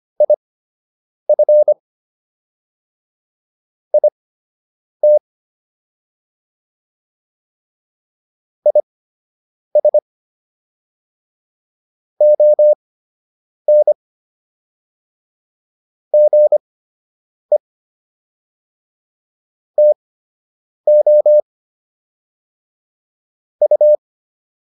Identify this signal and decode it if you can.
Morse code, heard as a keyed tone, about 25 wpm, spaced as for 8 wpm, 600 Hz. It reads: IF IT IS ON GE TO U